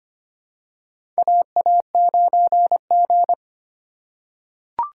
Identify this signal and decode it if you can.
Morse code, heard as a keyed tone, about 25 wpm, 700 Hz. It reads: AA9G
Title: USA Call Signs - 4 Characters - 4x Character Spacing - 25wpm